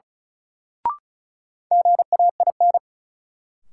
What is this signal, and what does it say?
Morse code, heard as a keyed tone, about 36 wpm, 700 Hz. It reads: GAIN
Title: Sentences from Top 1000 Words - New Word Drill - 35wpm